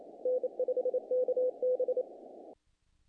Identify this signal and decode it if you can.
Morse code, heard as a keyed tone, about 26 wpm, 500 Hz. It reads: N5KB
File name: contests / naqp-cw-k5tr-op-k5pi- -1800